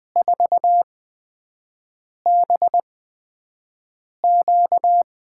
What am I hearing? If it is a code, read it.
Morse code, heard as a keyed tone, about 20 wpm, spaced as for 5 wpm, 700 Hz. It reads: 4BQ
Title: USA Call Signs - 6 Characters - 8x Character Spacing - 20wpm